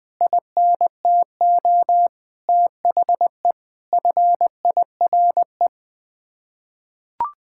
Morse code, heard as a keyed tone, about 20 wpm, 700 Hz. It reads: INTO THE FIRE
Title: Sets of 9 Words - 20wpm